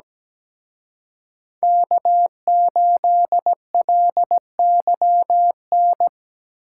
Morse code, heard as a keyed tone, about 20 wpm, 700 Hz. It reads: K8LYN
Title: Speed Racing - USA 1x3 Call Signs 17wpm